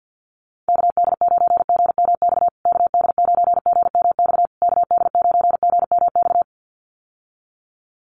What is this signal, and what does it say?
Morse code, heard as a keyed tone, about 50 wpm, 700 Hz. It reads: KD9GMX KD9GMX KD9GMX
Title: USA Call Signs Repeated 3x 50wpm